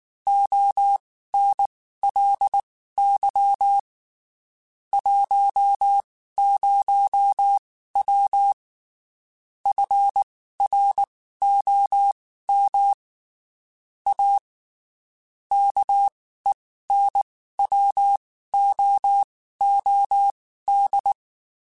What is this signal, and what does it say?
Morse code, heard as a keyed tone, about 19 wpm, spaced as for 14 wpm, 800 Hz. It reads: ONLY 10W FROM A KENWOOD
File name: (spaced) / QSOs Spaced-mp3s / kc7sac @18 sp